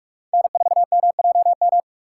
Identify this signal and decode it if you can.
Morse code, heard as a keyed tone, about 43 wpm, 700 Hz. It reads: N4MJM
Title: USA Call Signs - 5 Characters - 3x Character Spacing - 45wpm